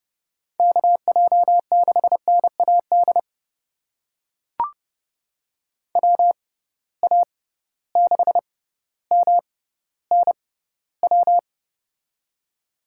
Morse code, heard as a keyed tone, about 30 wpm, spaced as for 15 wpm, 700 Hz. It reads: KJ6NAD WA6MNW
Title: USA Call Signs - 6 Characters - 6x Character Spacing - 30wpm